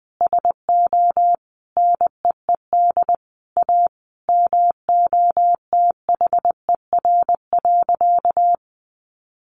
Morse code, heard as a keyed tone, about 20 wpm, 700 Hz. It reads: SO NEED A MOTHER.
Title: Sentences from Top 200 Words 20wpm